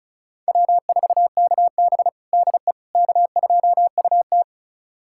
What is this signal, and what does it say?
Morse code, heard as a keyed tone, about 34 wpm, 700 Hz. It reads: W4KB DE K2UT
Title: USA Callsign DE Callsign - 4 Characters - 35wpm